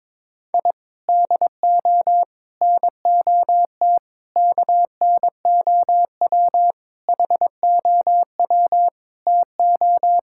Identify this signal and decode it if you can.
Morse code, heard as a keyed tone, about 22 wpm, 700 Hz. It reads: I DO NOT KNOW HOW TO
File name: practice / 22wpm / Sentences with 3-Word Phrases - 22wpm